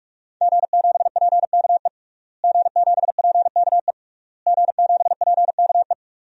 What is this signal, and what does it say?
Morse code, heard as a keyed tone, about 41 wpm, 700 Hz. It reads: G7PKE G7PKE G7PKE
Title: DXCC-100 Call Signs - 5 Characters - Repeated 3x - 45wpm